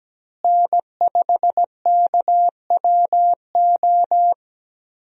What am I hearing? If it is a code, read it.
Morse code, heard as a keyed tone, about 17 wpm, 700 Hz. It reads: N5KWO
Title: USA Call Signs - 5 Characters - Repeated 3x - 17wpm